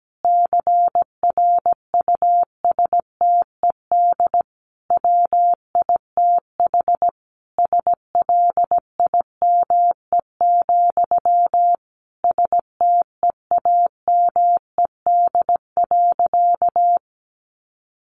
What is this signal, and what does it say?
Morse code, heard as a keyed tone, about 17 wpm, 700 Hz. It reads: CRUSTED WITH SLIME, STEAMED.
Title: A Sound Of Thunder 17wpm